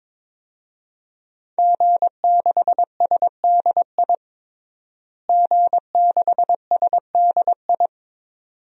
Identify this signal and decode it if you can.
Morse code, heard as a keyed tone, about 22 wpm, 700 Hz. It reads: G6SDI G6SDI
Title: DXCC-100 Call Signs - 5 Characters - Repeated 2x - 22wpm